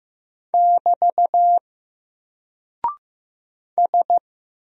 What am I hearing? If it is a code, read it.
Morse code, heard as a keyed tone, about 15 wpm, 700 Hz. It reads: <BT> S
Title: Single Character (Letter:Number:Punctuation:Prosign) 15wpm